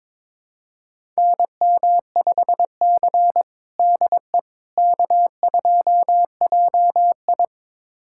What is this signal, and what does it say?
Morse code, heard as a keyed tone, about 22 wpm, 700 Hz. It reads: NM5C DE K2JI
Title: USA Callsign DE Callsign - 4 Characters - 22wpm